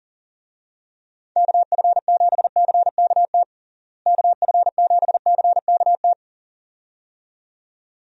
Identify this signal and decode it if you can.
Morse code, heard as a keyed tone, about 40 wpm, 700 Hz. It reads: KF7CKT KF7CKT
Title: USA Call Signs - 6 Characters - Repeated 2x - 40wpm